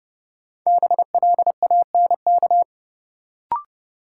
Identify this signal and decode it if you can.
Morse code, heard as a keyed tone, about 32 wpm, 700 Hz. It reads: BLANK
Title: Sentences from Top 1500 Words - New Word Drill Alt - 30wpm